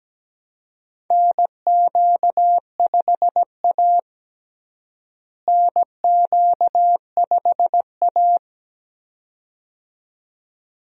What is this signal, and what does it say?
Morse code, heard as a keyed tone, about 17 wpm, 700 Hz. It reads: NQ5A NQ5A
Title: USA Call Signs - 4 Characters - Repeated 2x - 17wpm